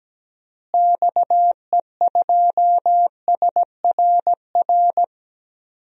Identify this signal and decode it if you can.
Morse code, heard as a keyed tone, about 17 wpm, 700 Hz. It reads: XE2SRR
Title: DXCC-100 Call Signs - 6 Characters - Repeated 3x - 17wpm